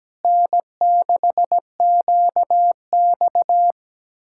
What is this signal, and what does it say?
Morse code, heard as a keyed tone, about 17 wpm, 700 Hz. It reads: N6QX
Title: Speed-Racing Common QSO Phrases Encore - 17wpm